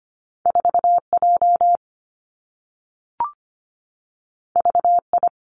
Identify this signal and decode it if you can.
Morse code, heard as a keyed tone, about 25 wpm, 700 Hz. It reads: 4J VI
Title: Two Challenging Letter-Number Combinations 25wpm